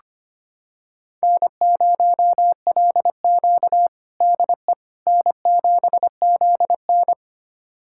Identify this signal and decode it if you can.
Morse code, heard as a keyed tone, about 25 wpm, 700 Hz. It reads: N0LQ DE N7ZN
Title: USA Callsign DE Callsign - 4 Characters - 25wpm